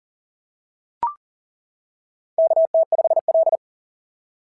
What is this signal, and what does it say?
Morse code, heard as a keyed tone, about 40 wpm, 660 Hz. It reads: KT5L
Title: SOTA QSO Practice 40wpm